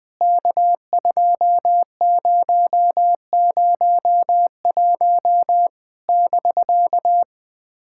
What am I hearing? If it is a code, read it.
Morse code, heard as a keyed tone, about 20 wpm, 700 Hz. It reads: K2001 <BK>
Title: POTA QSO Practice - 20wpm